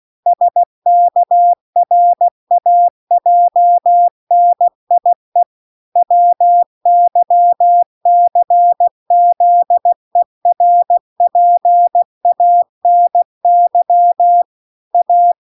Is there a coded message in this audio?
Yes: Morse code from 0.3 to 15.3 s, heard as a keyed tone, about 16 wpm, 700 Hz. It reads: SKRAJNIE WYCZERPANY A